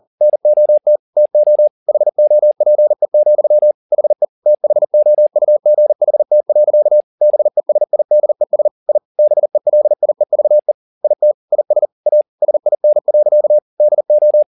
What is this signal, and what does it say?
Morse code, heard as a keyed tone, about 40 wpm, 600 Hz. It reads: NOT TO HOPE, HE THOUGHT. BESIDES I BELIEVE IT IS A SIN. DO
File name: the old man and the sea 2 40wpm